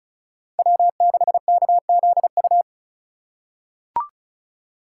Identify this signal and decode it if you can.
Morse code, heard as a keyed tone, about 34 wpm, 700 Hz. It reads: W6KZU
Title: USA Call Signs - 4x Character Spacing - 35wpm